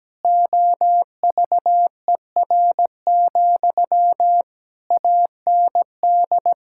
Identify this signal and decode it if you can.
Morse code, heard as a keyed tone, about 17 wpm, 700 Hz. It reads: OVER, AND
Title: Sentences from Top 1600 Words 17wpm